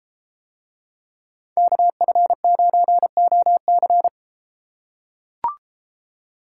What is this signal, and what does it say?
Morse code, heard as a keyed tone, about 33 wpm, 700 Hz. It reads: KF9OC